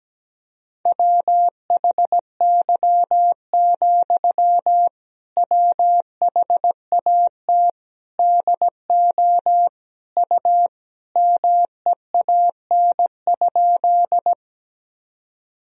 Morse code, heard as a keyed tone, about 17 wpm, 700 Hz. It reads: WHY, WHAT DO U MEAN?
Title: Sentences from Top 200 Words and Abbreviations 17wpm